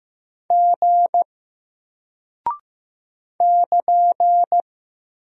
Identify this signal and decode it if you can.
Morse code, heard as a keyed tone, about 15 wpm, 700 Hz. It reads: G <KN>
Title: Single Character (Letter:Number:Punctuation:Prosign) Alt 15wpm